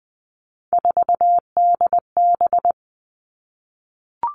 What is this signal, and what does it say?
Morse code, heard as a keyed tone, about 20 wpm, 700 Hz. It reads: 4DB
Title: All Cut Numbers - 3-Digit 20wpm